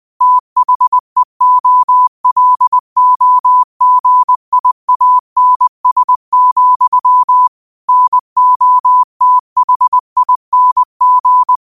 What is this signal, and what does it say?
Morse code, heard as a keyed tone, about 20 wpm, 1000 Hz. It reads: THEOLOGIANS, NOTHING
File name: qotdfiles / 20wpm